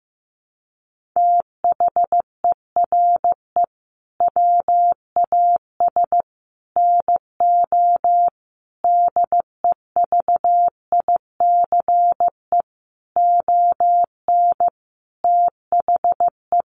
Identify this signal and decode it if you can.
Morse code, heard as a keyed tone, about 15 wpm, 700 Hz. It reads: THERE WAS NO DEVICE ON THE
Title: Sentences from Top 1500 Words 15wpm